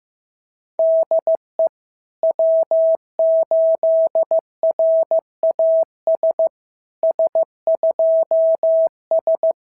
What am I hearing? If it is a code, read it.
Morse code, heard as a keyed tone, about 15 wpm, 650 Hz. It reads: DE W8RAS S2S
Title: SOTA QSO Practice 15wpm